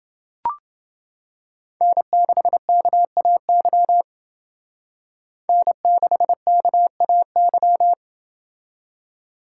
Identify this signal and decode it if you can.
Morse code, heard as a keyed tone, about 28 wpm, 700 Hz. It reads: N6KAY N6KAY